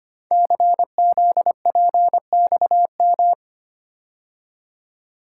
Morse code, heard as a keyed tone, about 25 wpm, 700 Hz. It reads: CZPXM
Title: Code Groups - 5 Letters - 25wpm